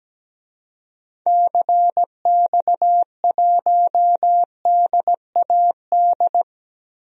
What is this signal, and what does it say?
Morse code, heard as a keyed tone, about 17 wpm, 700 Hz. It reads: CX1DAD
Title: DXCC-100 Call Signs - 6 Characters - Repeated 2x - 17wpm